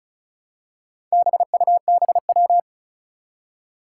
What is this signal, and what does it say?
Morse code, heard as a keyed tone, about 36 wpm, 700 Hz. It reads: BUBW
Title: Four Challenging Letter-Number Combinations 35wpm